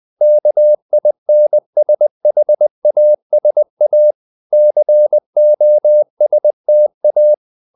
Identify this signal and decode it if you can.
Morse code, heard as a keyed tone, about 20 wpm, 600 Hz. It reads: KINSHASA COSTA
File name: CW / Practice / NationalCapitals20